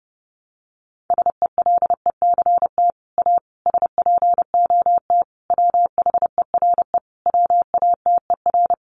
Morse code, heard as a keyed tone, about 30 wpm, 700 Hz. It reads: SELECT A SPOT WHERE WATER
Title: Sentences from Top 1300 Words 30wpm